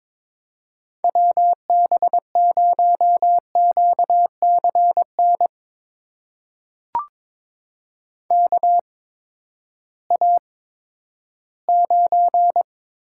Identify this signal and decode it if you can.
Morse code, heard as a keyed tone, about 22 wpm, spaced as for 12 wpm, 700 Hz. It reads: WB0QCN KA9
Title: USA Call Signs - 6 Characters - 8x Character Spacing - 22wpm